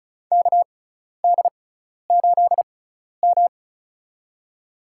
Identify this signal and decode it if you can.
Morse code, heard as a keyed tone, about 36 wpm, spaced as for 12 wpm, 700 Hz. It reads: KD8M
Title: USA Call Signs - 4 Characters - 6x Character Spacing - 35wpm